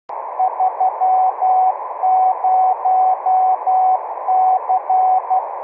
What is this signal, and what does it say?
Morse code, heard as a keyed tone, about 12 wpm, 750 Hz, with radio noise. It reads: 30C
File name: k7tmg / Temp2Morse